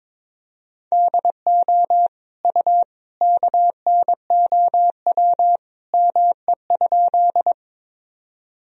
Morse code, heard as a keyed tone, about 22 wpm, 700 Hz. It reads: DO U KNOW ME?
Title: Sentences from Top 200 Words and Abbreviations 22wpm